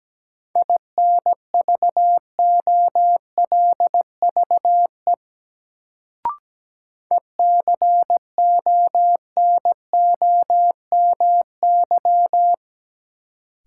Morse code, heard as a keyed tone, about 17 wpm, 700 Hz. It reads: INVOLVE ECONOMY
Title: Sentences from Top 1200 Words - New Word Drill Alt - 17wpm